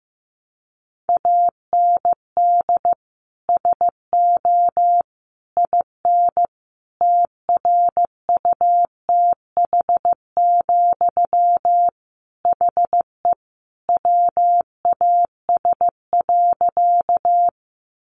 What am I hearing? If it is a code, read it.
Morse code, heard as a keyed tone, about 15 wpm, 700 Hz. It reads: AND SO IN TRUTH, HE WAS.
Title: Sentences from Top 800 Words 15wpm